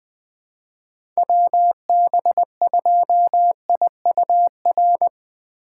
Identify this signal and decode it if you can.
Morse code, heard as a keyed tone, about 21 wpm, 700 Hz. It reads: WB2IUR
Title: USA Call Signs - 6 Characters - 2x Character Spacing - 20wpm